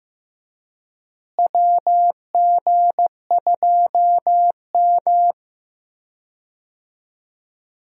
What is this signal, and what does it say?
Morse code, heard as a keyed tone, about 15 wpm, 700 Hz. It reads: WG2M
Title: USA Call Signs - 4 Characters - Repeated 2x - 15wpm